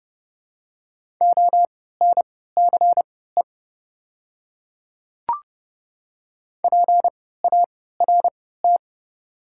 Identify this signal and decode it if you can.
Morse code, heard as a keyed tone, about 30 wpm, spaced as for 17 wpm, 700 Hz. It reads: ONCE PART
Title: 3x Character Spacing - Top 300 Words - 30wpm